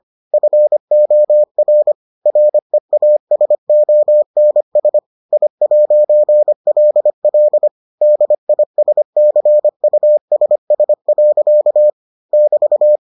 Morse code, heard as a keyed tone, about 25 wpm, 600 Hz. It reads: FOR REASONS I'LL DISCUSS. <BT>